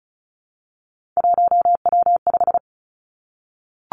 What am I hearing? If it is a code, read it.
Morse code, heard as a keyed tone, about 36 wpm, 700 Hz. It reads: 1W5
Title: Three Challenging Letter-Number Combinations 35wpm